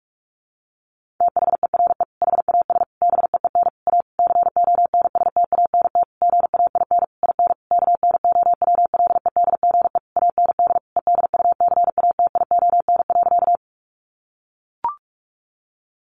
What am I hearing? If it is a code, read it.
Morse code, heard as a keyed tone, about 44 wpm, 700 Hz. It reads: THERE HAS BEEN A CONSTANT GAIN IN KNOWLEDGE AND EDUCATION.
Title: Sentences from Top 1500 Words 45wpm